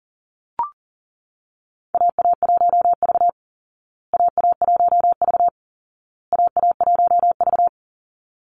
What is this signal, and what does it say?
Morse code, heard as a keyed tone, about 40 wpm, 700 Hz. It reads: AA1V AA1V AA1V
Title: Call Signs Repeated 2x - 40wpm